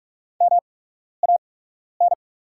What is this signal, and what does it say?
Morse code, heard as a keyed tone, about 44 wpm, spaced as for 12 wpm, 690 Hz. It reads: MAN
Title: 8x Character Spacing - Top 300 Words - 45wpm